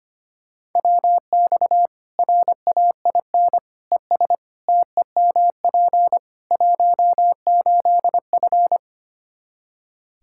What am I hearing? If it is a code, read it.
Morse code, heard as a keyed tone, about 25 wpm, 700 Hz. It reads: WX RAIN ES TEMP 18F
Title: K1USN SST QSO Practice 25wpm